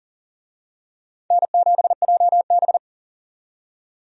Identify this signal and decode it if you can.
Morse code, heard as a keyed tone, about 40 wpm, 700 Hz. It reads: N7JB
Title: USA Call Signs - 4 Characters - 40wpm